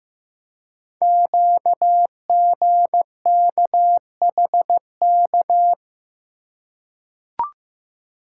Morse code, heard as a keyed tone, about 15 wpm, 700 Hz. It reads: QGKHK
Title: Code Groups - 5 Letters - 15wpm